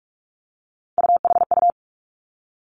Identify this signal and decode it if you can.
Morse code, heard as a keyed tone, about 46 wpm, 700 Hz. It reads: UHU
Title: Three Challenging Letter-Number Combinations 45wpm